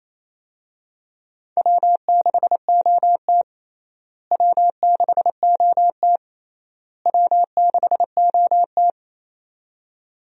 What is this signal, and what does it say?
Morse code, heard as a keyed tone, about 28 wpm, 700 Hz. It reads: W6OT W6OT W6OT
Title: USA Call Signs - 4 Characters - Repeated 3x - 28wpm